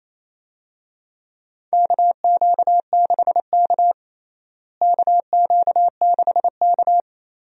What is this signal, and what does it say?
Morse code, heard as a keyed tone, about 28 wpm, 700 Hz. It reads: KQ6K KQ6K
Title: USA Call Signs - 4 Characters - Repeated 2x - 28wpm